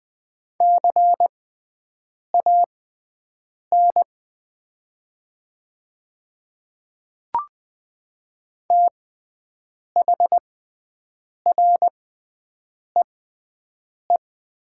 Morse code, heard as a keyed tone, about 20 wpm, spaced as for 7 wpm, 700 Hz. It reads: CAN THREE
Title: 6x Character Spacing - Top 200 Words - 20wpm